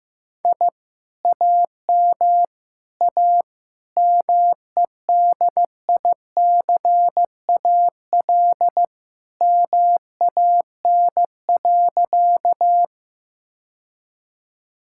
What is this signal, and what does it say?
Morse code, heard as a keyed tone, about 15 wpm, 700 Hz. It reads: I AM A MEDICAL MAN.
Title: Sentences from Top 1000 Words - Review Part 5 - 15wpm